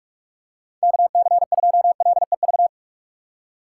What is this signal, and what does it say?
Morse code, heard as a keyed tone, about 46 wpm, 700 Hz. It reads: KC2LEV
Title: USA Call Signs - 6 Characters - 6x Character Spacing - 45wpm